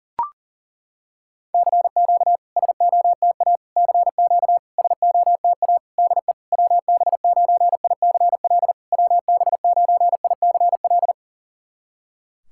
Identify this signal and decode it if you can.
Morse code, heard as a keyed tone, about 37 wpm, 700 Hz. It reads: CQ SOTA CQ SOTA DE WB9ICL WB9ICL
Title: SOTA QSO Practice 40wpm